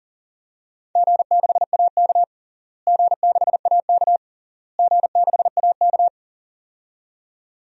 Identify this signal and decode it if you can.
Morse code, heard as a keyed tone, about 40 wpm, 700 Hz. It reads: G6AK G6AK G6AK
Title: DXCC-100 Call Signs - 4 Characters - Repeated 3x - 40wpm